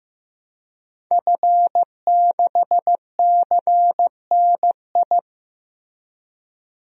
Morse code, heard as a keyed tone, about 15 wpm, 700 Hz. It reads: F6CNI
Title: DXCC-100 Call Signs - 5 Characters - Repeated 2x - 15wpm